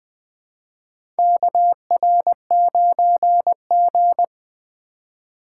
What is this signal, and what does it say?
Morse code, heard as a keyed tone, about 20 wpm, 700 Hz. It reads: KR9G
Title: USA Call Signs - 4 Characters - 4x Character Spacing - 20wpm